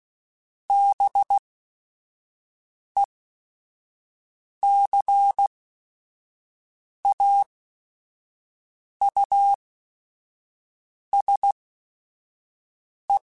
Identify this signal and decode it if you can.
Morse code, heard as a keyed tone, about 16 wpm, spaced as for 5 wpm, 800 Hz. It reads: BECAUSE
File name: Technical Subjects